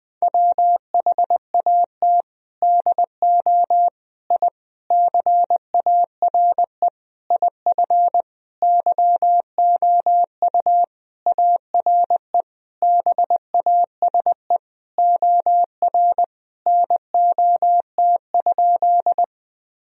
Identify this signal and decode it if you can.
Morse code, heard as a keyed tone, about 20 wpm, 700 Hz. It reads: WHAT DO I CARE IF YOU ARE BASE OR NOT?
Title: Sentences from Top 1000 Words - Review Part 1 - 20wpm